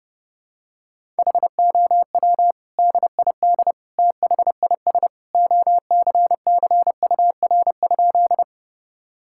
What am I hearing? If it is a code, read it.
Morse code, heard as a keyed tone, about 30 wpm, 700 Hz. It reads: HOW DID THIS OCCUR?
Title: Sentences from Top 1000 Words - Review Part 6 - 30wpm